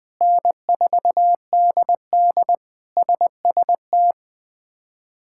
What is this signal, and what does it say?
Morse code, heard as a keyed tone, about 21 wpm, 700 Hz. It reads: N4DD SST
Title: K1USN SST QSO Practice - 20wpm